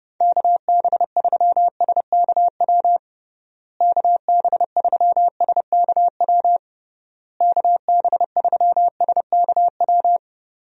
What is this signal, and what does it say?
Morse code, heard as a keyed tone, about 30 wpm, 700 Hz. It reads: KB3SKW KB3SKW KB3SKW
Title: USA Call Signs - 6 Characters - Repeated 3x - 30wpm